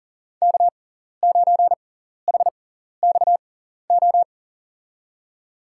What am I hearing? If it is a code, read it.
Morse code, heard as a keyed tone, about 40 wpm, spaced as for 14 wpm, 700 Hz. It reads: K9HXO
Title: USA Call Signs - 5 Characters - 6x Character Spacing - 40wpm